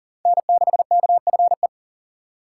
Morse code, heard as a keyed tone, about 37 wpm, 700 Hz. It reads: N6KFE
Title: USA Call Signs - 5 Characters - 2x Character Spacing - 40wpm